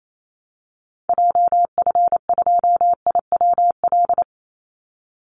Morse code, heard as a keyed tone, about 28 wpm, 700 Hz. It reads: JF2IWL
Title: Call Signs Repeated 3x - 28wpm